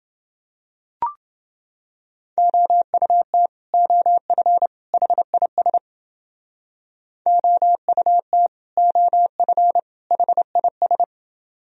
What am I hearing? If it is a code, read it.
Morse code, heard as a keyed tone, about 31 wpm, 700 Hz. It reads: OUT OF HIS OUT OF HIS